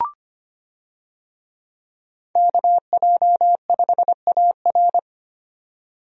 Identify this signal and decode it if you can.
Morse code, heard as a keyed tone, about 25 wpm, 700 Hz. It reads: KJ5AR
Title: USA Call Signs - 5 Characters - 25wpm